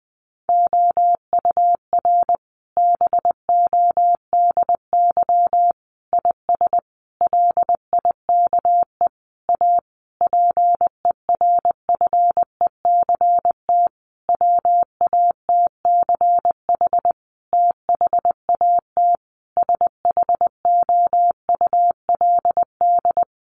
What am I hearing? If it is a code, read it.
Morse code, heard as a keyed tone, about 20 wpm, 700 Hz. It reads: OUR BODY IS LIKE A PERFECT WATCH THAT SHOULD
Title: Sentences from Top 1000 Words 20wpm